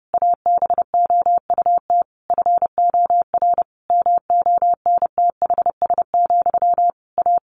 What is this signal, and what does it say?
Morse code, heard as a keyed tone, about 30 wpm, 700 Hz. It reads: ABOUT FOR MONTHS, A
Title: A Sound Of Thunder 30wpm